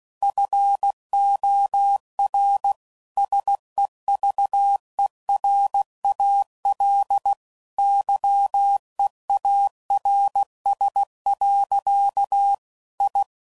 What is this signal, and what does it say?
Morse code, heard as a keyed tone, about 16 wpm, 800 Hz. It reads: FOR SEVERAL YEARS. I
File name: CW Code Groups @15WPM / QSOs / kc7haa